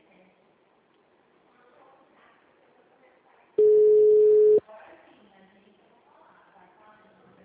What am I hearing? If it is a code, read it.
Ringback tone (Europe, 425 Hz), with speech.
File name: out- -2015- - - .69881